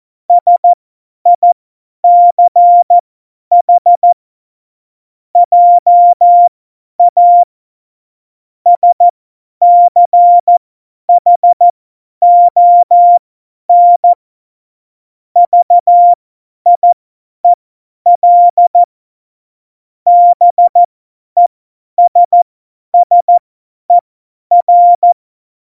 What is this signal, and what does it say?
Morse code, heard as a keyed tone, about 14 wpm, spaced as for 10 wpm, 700 Hz. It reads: SICH JA SCHON VIEL BESSER
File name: Morsekurs / Witze / Witz 12. Dezember 2017, 50 F70